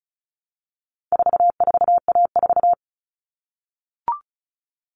Morse code, heard as a keyed tone, about 34 wpm, 700 Hz. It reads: 44A4
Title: All Cut Numbers - 4-Digit 35wpm